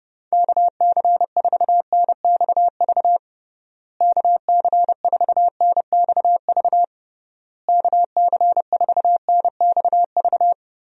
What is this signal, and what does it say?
Morse code, heard as a keyed tone, about 30 wpm, 700 Hz. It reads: KC4NXV KC4NXV KC4NXV
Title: USA Call Signs - 6 Characters - Repeated 3x - 30wpm